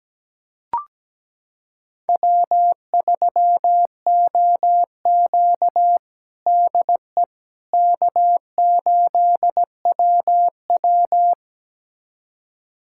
Morse code, heard as a keyed tone, about 17 wpm, 700 Hz. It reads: W3OQ DE K8WW
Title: USA Callsign DE Callsign - 4 Characters - 17wpm